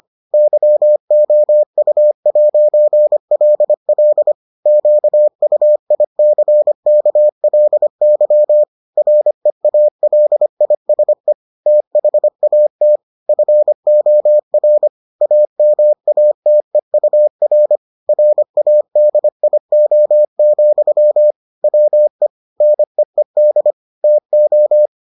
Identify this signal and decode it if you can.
Morse code, heard as a keyed tone, about 25 wpm, 600 Hz. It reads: YOU'LL QUICKLY REALISE THAT FOR AMATEUR RADIO, WE NEED TO